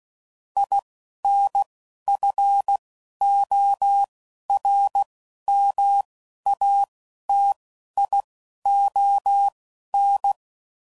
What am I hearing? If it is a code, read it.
Morse code, heard as a keyed tone, about 16 wpm, spaced as for 11 wpm, 800 Hz. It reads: INFORMATION
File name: CW Code Groups @10WPM / Technical Subjects / SwPwrSup